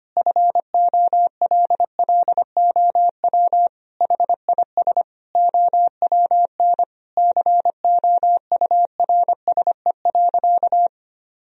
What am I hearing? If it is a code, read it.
Morse code, heard as a keyed tone, about 25 wpm, 700 Hz. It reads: FOLLOW HIS OWN COURSE.
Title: Sentences from Top 1000 Words - Review Part 1 - 25wpm